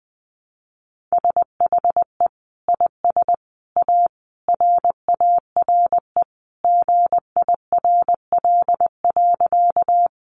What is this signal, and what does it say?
Morse code, heard as a keyed tone, about 19 wpm, 700 Hz. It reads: SHE IS A RARE GIRL.
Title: Sentences from Top 1100 Words 20wpm